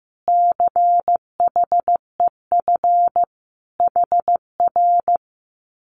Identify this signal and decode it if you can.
Morse code, heard as a keyed tone, about 15 wpm, 700 Hz. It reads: CHEF HR
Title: Common QSO Phrases Encore 15wpm